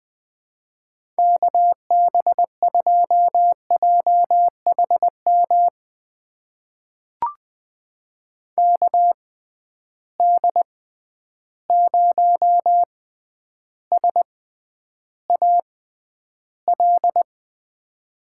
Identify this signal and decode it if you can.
Morse code, heard as a keyed tone, about 21 wpm, spaced as for 10 wpm, 700 Hz. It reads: KB2JHM KD0SAL
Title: USA Call Signs - 6 Characters - 6x Character Spacing - 20wpm